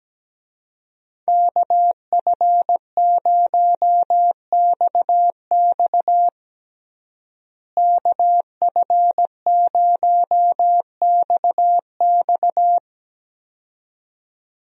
Morse code, heard as a keyed tone, about 17 wpm, 700 Hz. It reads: KF0XX KF0XX
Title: USA Call Signs - 5 Characters - Repeated 2x - 17wpm